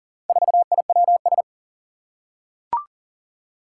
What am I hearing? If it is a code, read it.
Morse code, heard as a keyed tone, about 38 wpm, 700 Hz. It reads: 4IWS
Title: Four Challenging Letter-Number Combinations 40wpm